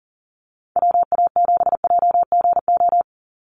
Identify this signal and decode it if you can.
Morse code, heard as a keyed tone, about 40 wpm, 700 Hz. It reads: WA7JGO
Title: USA Call Signs 40wpm